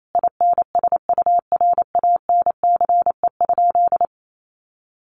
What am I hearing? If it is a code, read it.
Morse code, heard as a keyed tone, about 28 wpm, 700 Hz. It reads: INSURANCE?
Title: Sentences from Top 1100 Words 28wpm